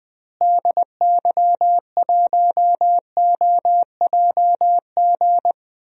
Morse code, heard as a keyed tone, about 20 wpm, 700 Hz. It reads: DY1OJG
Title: DXCC-100 Call Signs - 6 Characters - Repeated 2x - 20wpm